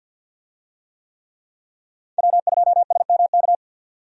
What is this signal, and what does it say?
Morse code, heard as a keyed tone, about 52 wpm, 700 Hz. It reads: W2SMK
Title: USA Call Signs - 5 Characters - 50wpm